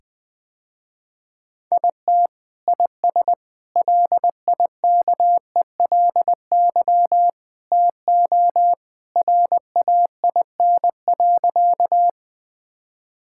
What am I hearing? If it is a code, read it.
Morse code, heard as a keyed tone, about 21 wpm, 700 Hz. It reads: IT IS LIKELY TO RAIN.
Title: Sentences with 2-Word Phrases - Spoken Beginning - 20wpm